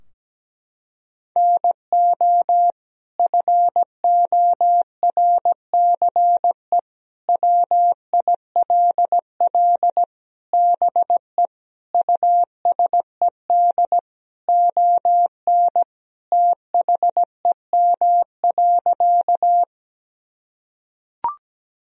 Morse code, heard as a keyed tone, about 17 wpm, 700 Hz. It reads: NO FORCE WILL BE USED ON THEM.
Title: Sentences from Top 1000 Words - Review Part 3 - 17wpm